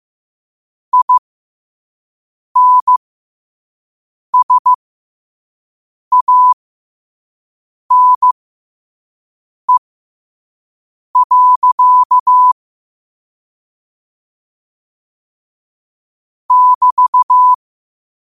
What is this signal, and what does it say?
Morse code, heard as a keyed tone, about 15 wpm, spaced as for 5 wpm, 1000 Hz. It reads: INSANE. <BT>
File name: qotdfiles / 5wpm